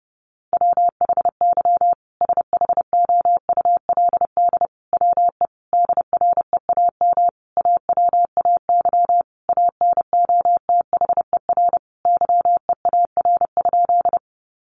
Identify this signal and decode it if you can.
Morse code, heard as a keyed tone, about 30 wpm, 700 Hz. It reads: WHY SHOULD WE DREAM AWAY ANOTHER YEAR?
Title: Sentences from Top 1500 Words 30wpm